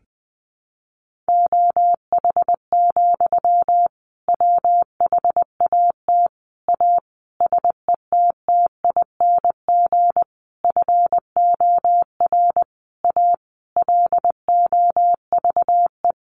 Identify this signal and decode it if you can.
Morse code, heard as a keyed tone, about 19 wpm, 700 Hz. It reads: OH, WHAT A SETTING FOR A LOVE
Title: Sentences from Top 1300 Words 20wpm